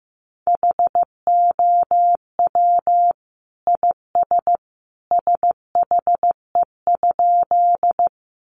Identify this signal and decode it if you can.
Morse code, heard as a keyed tone, about 14 wpm, 700 Hz. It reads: HOW IS SHE?
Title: Sentences from Top 100 Words 15wpm